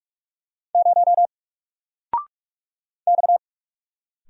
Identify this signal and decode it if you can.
Morse code, heard as a keyed tone, about 46 wpm, 700 Hz. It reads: 0 X
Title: Single Letter-Number Alt 45wpm